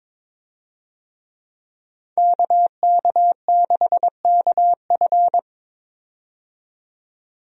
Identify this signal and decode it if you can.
Morse code, heard as a keyed tone, about 22 wpm, 700 Hz. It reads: KK6KF
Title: USA Call Signs - 5 Characters - 22wpm